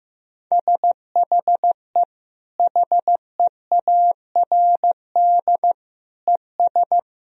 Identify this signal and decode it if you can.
Morse code, heard as a keyed tone, about 15 wpm, 700 Hz. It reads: SHE HEARD ES
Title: Sentences from Top 500 Words and Abbreviations 15wpm